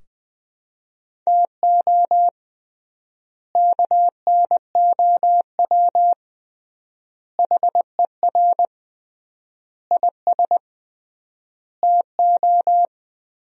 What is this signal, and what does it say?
Morse code, heard as a keyed tone, about 21 wpm, 700 Hz. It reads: TO KNOW HER IS TO
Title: 3x Word-Spacing - Sentences from Top 400 Words - 20wpm